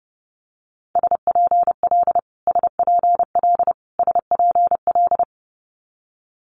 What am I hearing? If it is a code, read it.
Morse code, heard as a keyed tone, about 30 wpm, 700 Hz. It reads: SPL SPL SPL
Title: Multi-letter Phonemes Repeated 3x 30wpm